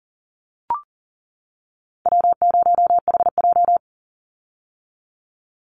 Morse code, heard as a keyed tone, about 40 wpm, 700 Hz. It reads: W0HJ
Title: USA Call Signs 40wpm